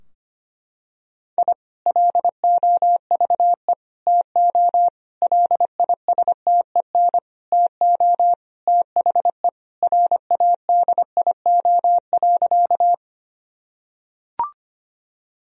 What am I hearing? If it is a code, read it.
Morse code, heard as a keyed tone, about 25 wpm, 700 Hz. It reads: I LOVE TO LISTEN TO THE RADIO.
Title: Sentences from Top 1000 Words - Review Part 3 - 25wpm